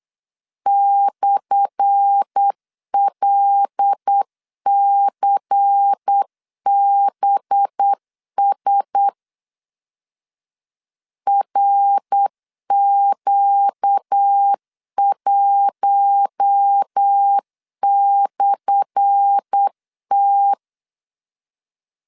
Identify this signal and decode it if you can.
Morse code, heard as a keyed tone, about 8 wpm, 800 Hz. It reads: /LCBS RQ1/T
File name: LEZIONI / 10 SIGNS / 10A SLASH 8wpm 3452